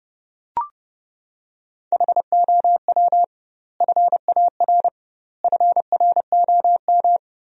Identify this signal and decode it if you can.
Morse code, heard as a keyed tone, about 30 wpm, 700 Hz. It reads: HOW FAR FROM
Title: 2x Word-Spacing - Sentences from Top 400 Words - 30wpm